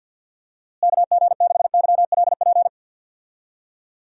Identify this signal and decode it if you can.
Morse code, heard as a keyed tone, about 46 wpm, 700 Hz. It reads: KG6YLP
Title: USA Call Signs - 6 Characters - Repeated 3x - 50wpm